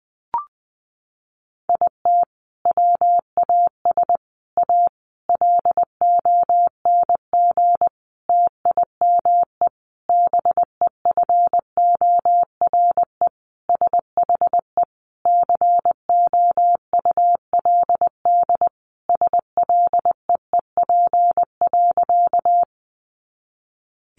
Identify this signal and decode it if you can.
Morse code, heard as a keyed tone, about 20 wpm, 700 Hz. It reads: IT WAS A LONG TIME BEFORE SHE COULD SLEEP.
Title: Sentences from Top 500 Words 20wpm